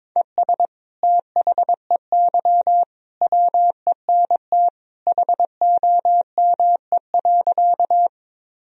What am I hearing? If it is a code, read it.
Morse code, heard as a keyed tone, about 22 wpm, 700 Hz. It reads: ES THEY WENT HOME.
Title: Sentences from Top 200 Words and Abbreviations 22wpm